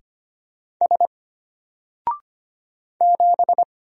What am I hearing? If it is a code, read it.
Morse code, heard as a keyed tone, about 25 wpm, 700 Hz. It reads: S 7
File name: Single-Letter-Number-25wpm